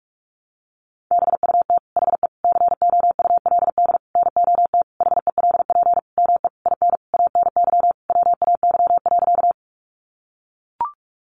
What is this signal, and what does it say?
Morse code, heard as a keyed tone, about 44 wpm, 700 Hz. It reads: BUT HE COULD NOT HELP ME IN ANY WAY.
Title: Sentences from Top 600 Words 45wpm